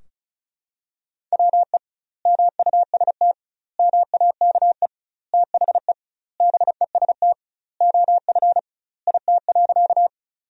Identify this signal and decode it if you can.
Morse code, heard as a keyed tone, about 36 wpm, 700 Hz. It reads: WE MUST MAKE THE BEST OF IT.
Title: 2x Word-Spacing - Sentences from Top 400 Words - 35wpm